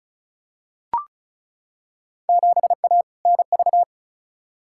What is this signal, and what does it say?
Morse code, heard as a keyed tone, about 36 wpm, 700 Hz. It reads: 7A NV
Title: Field Day Exchanges Encore - 35wpm